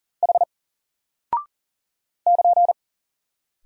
Morse code, heard as a keyed tone, about 40 wpm, 700 Hz. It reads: H <KN>
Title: Single Character (Letter:Number:Punctuation:Prosign) 40wpm